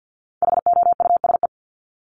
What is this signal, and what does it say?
Morse code, heard as a keyed tone, about 50 wpm, 700 Hz. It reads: HOUSE
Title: Top 200 Words Repeated 50wpm